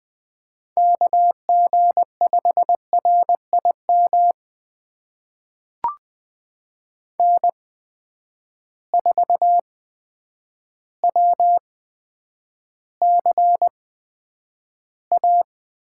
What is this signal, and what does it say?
Morse code, heard as a keyed tone, about 20 wpm, spaced as for 9 wpm, 700 Hz. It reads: KG5RIM N4WCA
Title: USA Call Signs - 8x Character Spacing - 20wpm